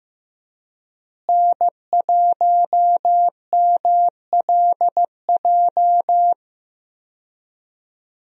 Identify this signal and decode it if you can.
Morse code, heard as a keyed tone, about 15 wpm, 700 Hz. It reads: N1MLJ
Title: USA Call Signs - 5 Characters - Repeated 3x - 15wpm